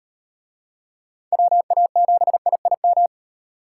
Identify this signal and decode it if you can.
Morse code, heard as a keyed tone, about 38 wpm, 700 Hz. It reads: WA7IIM